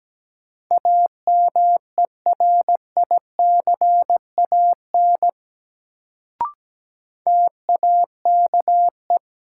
Morse code, heard as a keyed tone, about 17 wpm, 700 Hz. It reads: AMERICAN TAKE
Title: Sets of 2 Words - New Word Drill - 17wpm